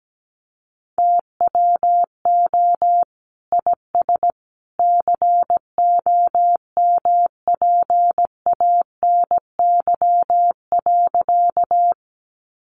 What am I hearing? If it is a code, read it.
Morse code, heard as a keyed tone, about 17 wpm, 700 Hz. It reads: TWO IS COMPANY.
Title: Sentences from Top 600 Words 17wpm